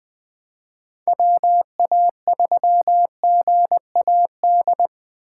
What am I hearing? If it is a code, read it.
Morse code, heard as a keyed tone, about 20 wpm, 700 Hz. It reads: WA3GAD
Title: Speed Racing - USA 2x3 Call Signs 17wpm